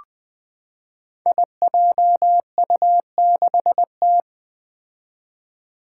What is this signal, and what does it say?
Morse code, heard as a keyed tone, about 20 wpm, 700 Hz. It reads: IJU6T
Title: Five Challenging Letter-Number Combinations 20wpm